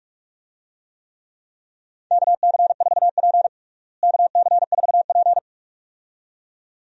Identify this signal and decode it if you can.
Morse code, heard as a keyed tone, about 46 wpm, 700 Hz. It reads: KC4P KC4P
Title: USA Call Signs - 4 Characters - Repeated 2x - 45wpm